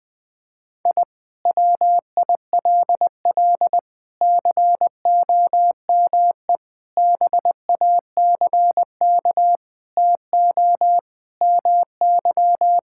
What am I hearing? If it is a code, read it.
Morse code, heard as a keyed tone, about 21 wpm, 700 Hz. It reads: I WILL COME BACK TO MY
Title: Sentences from Top 1000 Words - Review Part 6 - 20wpm